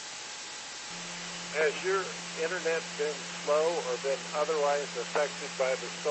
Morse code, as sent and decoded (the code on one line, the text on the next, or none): none